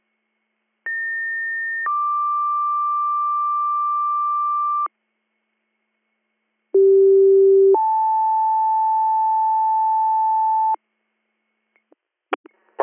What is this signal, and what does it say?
Two-tone page, heard as a one-second tone, then a three-second tone, about 1800 then 1180 Hz, about 390 then 880 Hz. It is heard from 0.9 to 10.8 s.